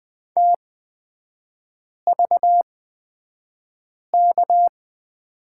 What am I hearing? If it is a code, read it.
Morse code, heard as a keyed tone, about 20 wpm, spaced as for 5 wpm, 700 Hz. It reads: TVK
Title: 236 Lesson 38e - Letter q - Three Characters 20wpm